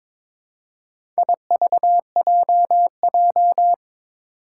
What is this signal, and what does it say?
Morse code, heard as a keyed tone, about 22 wpm, 700 Hz. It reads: IVJJ
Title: Four Challenging Letter-Number Combinations 22wpm